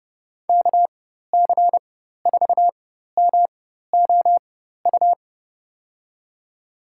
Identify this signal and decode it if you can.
Morse code, heard as a keyed tone, about 30 wpm, spaced as for 14 wpm, 700 Hz. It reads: KC4MOU
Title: USA Call Signs - 6 Characters - 4x Character Spacing - 30wpm